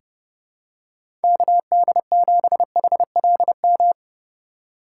Morse code, heard as a keyed tone, about 30 wpm, 700 Hz. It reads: KD7HLM
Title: USA Call Signs - 6 Characters - 3x Character Spacing - 30wpm